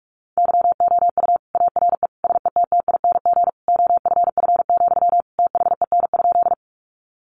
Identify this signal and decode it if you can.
Morse code, heard as a keyed tone, about 44 wpm, 700 Hz. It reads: YOU ARE SETTING OFF, THEN?
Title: Sentences from Top 1300 Words 45wpm